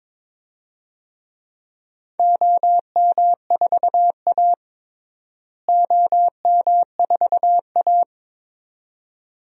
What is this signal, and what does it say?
Morse code, heard as a keyed tone, about 22 wpm, 700 Hz. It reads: OM4A OM4A
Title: DXCC-100 Call Signs - 4 Characters - Repeated 2x - 22wpm